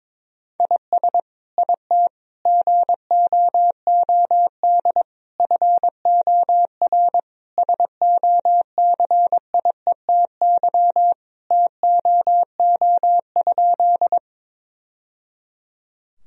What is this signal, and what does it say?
Morse code, heard as a keyed tone, about 22 wpm, 700 Hz. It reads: IS IT GOOD FOR SOCIETY TOO?
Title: Sentences from Top 1000 Words - Review Part 4 - 22wpm